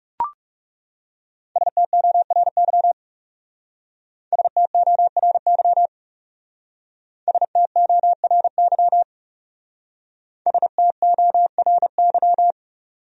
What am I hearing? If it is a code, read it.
Morse code, heard as a keyed tone, about 38 wpm, 700 Hz. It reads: STORY STORY STORY STORY